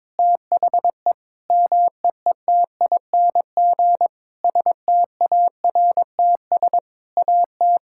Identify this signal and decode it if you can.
Morse code, heard as a keyed tone, about 22 wpm, 700 Hz. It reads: THE MEETING STARTS AT
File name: Sentences with 7-Word Phrases - 22wpm